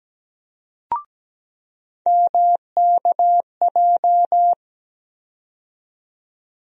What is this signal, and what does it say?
Morse code, heard as a keyed tone, about 17 wpm, 700 Hz. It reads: MKJ
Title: USA Call Sign Postfixes - 3 Letters 17wpm